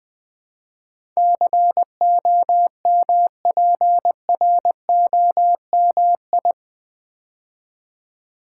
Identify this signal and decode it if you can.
Morse code, heard as a keyed tone, about 20 wpm, 700 Hz. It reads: COMPROMI
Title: Build A Word - 4 Letters At A Time - 10-Letter Words - 20wpm